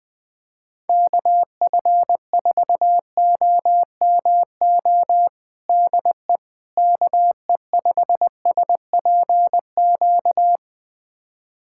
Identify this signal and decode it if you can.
Morse code, heard as a keyed tone, about 20 wpm, 700 Hz. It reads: KF4OMO DE KE5SPQ
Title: USA Callsign DE Callsign - 6 Characters - 20wpm